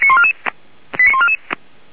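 Five-tone selective call (ZVEI), heard as quick runs of five tones, tones of about 70 ms. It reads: 89144
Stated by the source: Five-tone ZVEI selective call received by SDR.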